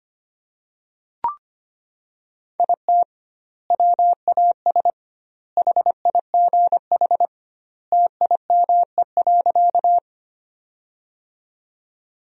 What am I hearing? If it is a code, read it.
Morse code, heard as a keyed tone, about 25 wpm, 700 Hz. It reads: IT WAS HIGH TIME.
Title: 2x Word-Spacing - Sentences from Top 200 Words - 25wpm